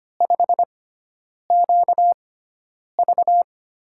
Morse code, heard as a keyed tone, about 25 wpm, spaced as for 9 wpm, 700 Hz. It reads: 5QV
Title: USA Call Signs - 6x Character Spacing - 25wpm